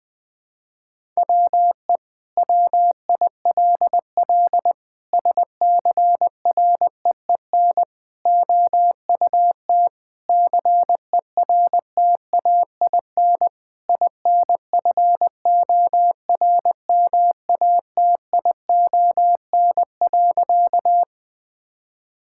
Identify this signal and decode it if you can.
Morse code, heard as a keyed tone, about 20 wpm, 700 Hz. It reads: WE WILL SCREEN OUT CERTAIN INFORMATION.
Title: Sentences from Top 1000 Words - Review Part 6 - 20wpm